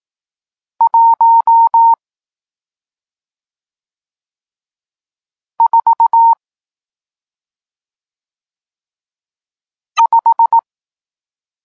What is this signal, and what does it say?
Morse code, heard as a keyed tone, about 18 wpm, 920 Hz. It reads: 1 4 5
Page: Tekst 40 – Nauka znaków przestankowych